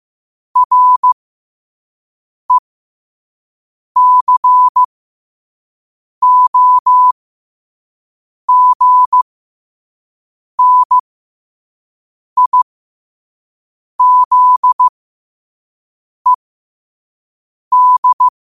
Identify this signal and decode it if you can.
Morse code, heard as a keyed tone, about 15 wpm, spaced as for 5 wpm, 1000 Hz. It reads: RECOGNIZED
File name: qotdfiles / 5wpm